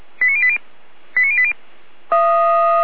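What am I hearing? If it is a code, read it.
Five-tone selective call (ZVEI), heard as quick runs of five tones, tones of about 80 ms. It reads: 89080 89080
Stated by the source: ZVEI five-tone call received by SDR.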